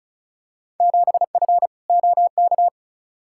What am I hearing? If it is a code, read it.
Morse code, heard as a keyed tone, about 36 wpm, 700 Hz. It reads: 7F OK
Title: Field Day Exchanges Encore - 35wpm